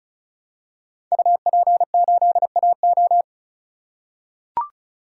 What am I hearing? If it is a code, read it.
Morse code, heard as a keyed tone, about 36 wpm, 700 Hz. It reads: UP8AO
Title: Code Groups - 5 Letter:Numbers - 35wpm